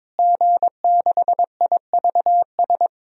Morse code, heard as a keyed tone, about 22 wpm, 700 Hz. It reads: G6IVS
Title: DXCC-100 Call Signs - 5 Characters - Repeated 3x - 22wpm